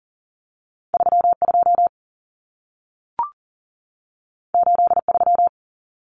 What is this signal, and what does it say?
Morse code, heard as a keyed tone, about 40 wpm, 700 Hz. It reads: 3 2 8 3
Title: Common Cut Numbers - 2-Digit 40wpm